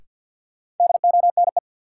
Morse code, heard as a keyed tone, about 49 wpm, 700 Hz. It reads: DONE
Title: Sets of 5 Words Encore - New Word Drill - 50wpm